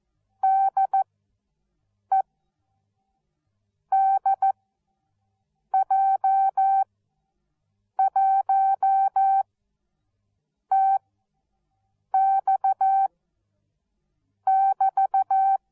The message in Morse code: DE DJ1TX<BT>